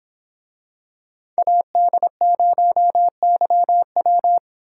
Morse code, heard as a keyed tone, about 26 wpm, 700 Hz. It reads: AD0YW